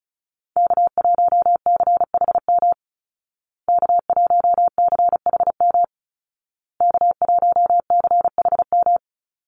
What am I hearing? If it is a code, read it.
Morse code, heard as a keyed tone, about 34 wpm, 700 Hz. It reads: K1CHM K1CHM K1CHM
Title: Call Signs Repeated 2x - 35wpm